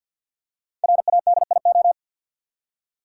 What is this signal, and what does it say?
Morse code, heard as a keyed tone, about 53 wpm, 700 Hz. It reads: RADIO
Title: Five Letter Words - 50wpm